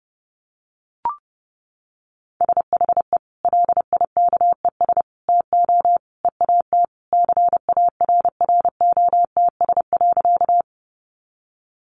Morse code, heard as a keyed tone, about 30 wpm, 700 Hz. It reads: SHE LIKES TO EAT CARROTS.
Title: Easy Phrases 30wpm